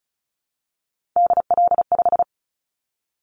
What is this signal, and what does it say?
Morse code, heard as a keyed tone, about 34 wpm, 700 Hz. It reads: DL5
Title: Callsign Prefixes for Top 50 DXCC - 35wpm